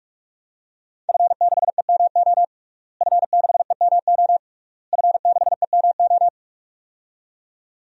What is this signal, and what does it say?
Morse code, heard as a keyed tone, about 46 wpm, 700 Hz. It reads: F6EMO F6EMO F6EMO
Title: DXCC-100 Call Signs - 5 Characters - Repeated 3x - 45wpm